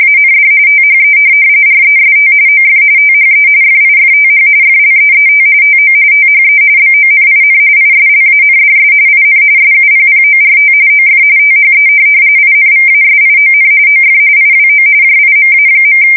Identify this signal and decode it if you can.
RTTY, about 45.45 baud. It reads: AND THE REST OF THE FIFTH FLEET TO THE BONIN ISLANDS IN SUPPORT OF THE INVASION OF IWO JIMA.
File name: fsk170 / USS-New-Jersey